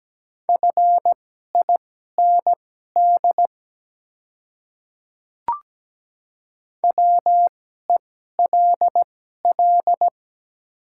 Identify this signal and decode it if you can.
Morse code, heard as a keyed tone, about 17 wpm, spaced as for 12 wpm, 700 Hz. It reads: FIND WELL
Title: 2x Character Spacing - Top 300 Words - 17wpm